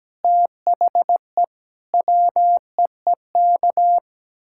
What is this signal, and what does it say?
Morse code, heard as a keyed tone, about 17 wpm, 700 Hz. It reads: THE WEEK
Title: Sentences from Top 1000 Words - Review Part 6 - 17wpm